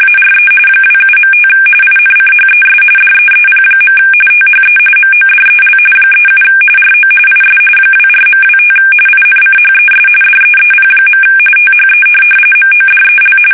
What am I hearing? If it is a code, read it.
RTTY, about 45.45 baud. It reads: ARE 27.5 INCHES DEEP. THE MODEL 37 RO IS 24.25 INCHES DEEP. THE MODEL
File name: fsk850 / teletype-corp